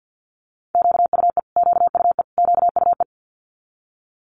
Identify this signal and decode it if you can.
Morse code, heard as a keyed tone, about 50 wpm, 700 Hz. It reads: QUE QUE QUE
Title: Multi-letter Phonemes Repeated 3x 50wpm